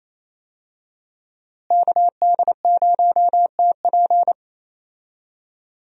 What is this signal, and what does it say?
Morse code, heard as a keyed tone, about 28 wpm, 700 Hz. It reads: KD0TP
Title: USA Call Signs - 5 Characters - Repeated 3x - 28wpm